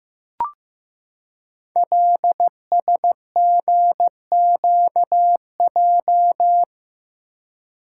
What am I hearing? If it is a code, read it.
Morse code, heard as a keyed tone, about 15 wpm, 700 Hz. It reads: LSGQJ
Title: Code Groups - 5 Letter:Numbers - 15wpm